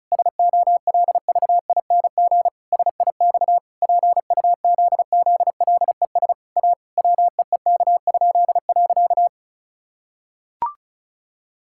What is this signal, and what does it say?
Morse code, heard as a keyed tone, about 36 wpm, 700 Hz. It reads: SOLVING SIX PUZZLES A WEEK?.
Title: Pangram Challenge v2 - 35wpm